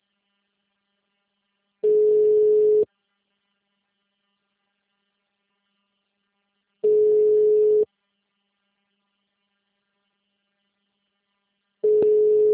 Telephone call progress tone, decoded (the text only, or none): ringback tone